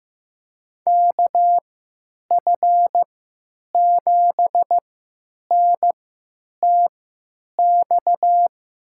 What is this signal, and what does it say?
Morse code, heard as a keyed tone, about 15 wpm, spaced as for 9 wpm, 700 Hz. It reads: KF7NTX
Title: USA Call Signs - 3x Character Spacing - 15wpm